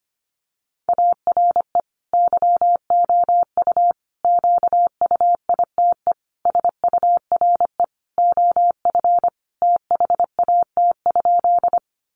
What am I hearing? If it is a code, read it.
Morse code, heard as a keyed tone, about 25 wpm, 700 Hz. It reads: ARE YOU QUITE SURE OF THAT?
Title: Sentences from Top 700 Words 25wpm